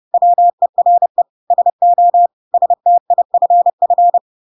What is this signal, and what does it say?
Morse code, heard as a keyed tone, about 30 wpm, 700 Hz. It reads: WERE SO STIFF
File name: Pinocchio / Pinocchio ch3 30 30 ik1hge